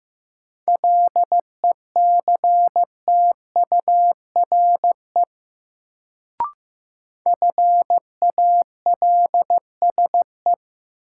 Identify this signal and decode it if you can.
Morse code, heard as a keyed tone, about 15 wpm, 700 Hz. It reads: LECTURE FALSE
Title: Sentences from Top 1600 Words - New Word Drill Alt - 15wpm